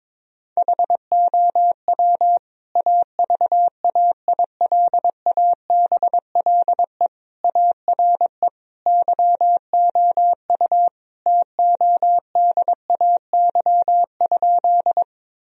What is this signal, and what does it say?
Morse code, heard as a keyed tone, about 22 wpm, 700 Hz. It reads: HOW AVAILABLE ARE YOU TODAY?
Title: Sentences from Top 1000 Words - Review Part 3 - 22wpm